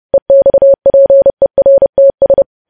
Morse code, heard as a keyed tone, about 30 wpm, 550 Hz. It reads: EXPERTS
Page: news headlines in morse code at 30 wpm